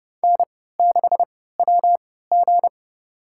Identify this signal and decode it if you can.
Morse code, heard as a keyed tone, about 30 wpm, spaced as for 17 wpm, 700 Hz. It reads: N6WG
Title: USA Call Signs - 4 Characters - 3x Character Spacing - 30wpm